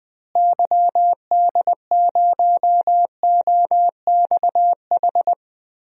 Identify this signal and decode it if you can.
Morse code, heard as a keyed tone, about 20 wpm, 700 Hz. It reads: YD0OXH
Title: DXCC-100 Call Signs - 6 Characters - Repeated 2x - 20wpm